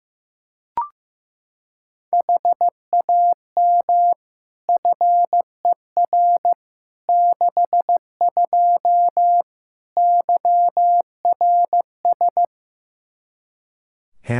Morse code, heard as a keyed tone, about 15 wpm, 700 Hz. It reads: HAM FER 62 YRS
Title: K1USN SST QSO Practice 15wpm